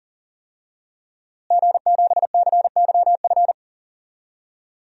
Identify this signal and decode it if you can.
Morse code, heard as a keyed tone, about 37 wpm, 700 Hz. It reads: G7CYF
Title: DXCC-100 Call Signs - 5 Characters - Repeated 2x - 40wpm